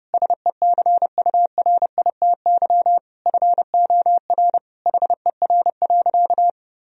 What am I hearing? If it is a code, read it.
Morse code, heard as a keyed tone, about 30 wpm, 700 Hz. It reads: SECURITY FOR HER.
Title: Sentences from Top 1000 Words - Review Part 4 - 30wpm